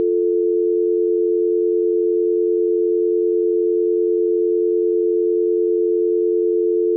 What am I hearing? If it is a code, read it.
Dial tone (North America), 350 + 440 Hz.
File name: Contents / dialtone